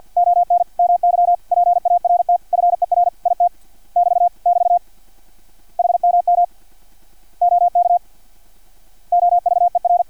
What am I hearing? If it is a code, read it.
Morse code, heard as a keyed tone, about 50 wpm, 700 Hz. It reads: ON MY PART FER IT <BT> <BT> HMM OK OVER
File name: archived / CRSnet